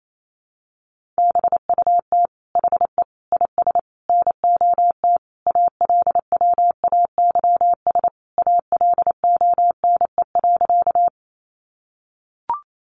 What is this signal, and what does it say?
Morse code, heard as a keyed tone, about 28 wpm, 700 Hz. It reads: BUT HE IS NOT ALWAYS ALONE.
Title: Sentences from Top 1000 Words 28wpm